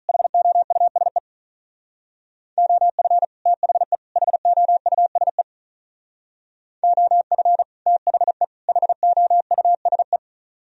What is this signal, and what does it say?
Morse code, heard as a keyed tone, about 44 wpm, 700 Hz. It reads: HOUSE OF THE HOUSE OF THE HOUSE